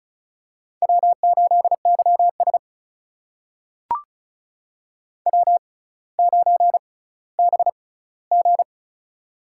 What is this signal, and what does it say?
Morse code, heard as a keyed tone, about 36 wpm, spaced as for 18 wpm, 700 Hz. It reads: W8YS W9BG
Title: USA Call Signs - 4 Characters - 6x Character Spacing - 35wpm